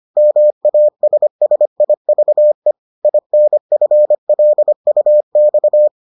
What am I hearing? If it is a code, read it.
Morse code, heard as a keyed tone, about 25 wpm, 600 Hz. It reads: MASSIVE INFLUX